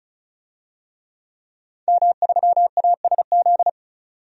Morse code, heard as a keyed tone, about 36 wpm, 700 Hz. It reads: M3ASZ
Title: DXCC-100 Call Signs - 5 Characters - 35wpm